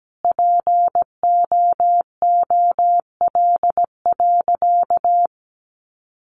Morse code, heard as a keyed tone, about 17 wpm, 700 Hz. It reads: POOL.
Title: Easy Phrases 17wpm